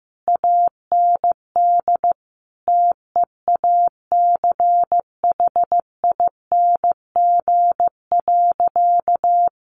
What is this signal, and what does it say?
Morse code, heard as a keyed tone, about 15 wpm, 700 Hz. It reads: AND TEACHING.
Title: Sentences from Top 1100 Words 15wpm